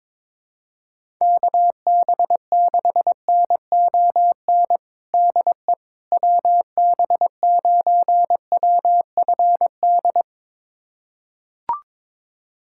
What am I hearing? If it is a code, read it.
Morse code, heard as a keyed tone, about 22 wpm, 700 Hz. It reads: KB6NON DE WB9WFD
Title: USA Callsign DE Callsign 22wpm